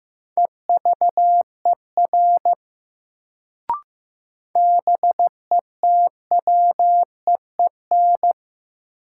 Morse code, heard as a keyed tone, about 15 wpm, 700 Hz. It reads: EVER BETWEEN
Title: Sentences from Top 400 Words - New Word Drill - 15wpm